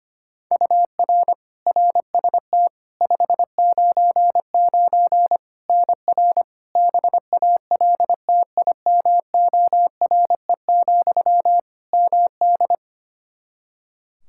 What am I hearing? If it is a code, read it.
Morse code, heard as a keyed tone, about 25 wpm, 700 Hz. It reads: UR RST 599 NR BALTIMORE, MD
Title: Practice QSO - Exchange 1 - Beginning - 25wpm